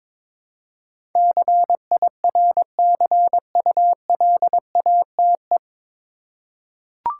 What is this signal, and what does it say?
Morse code, heard as a keyed tone, about 22 wpm, 700 Hz. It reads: CIRCULATE
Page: Morse Code Ninja practice 22wpm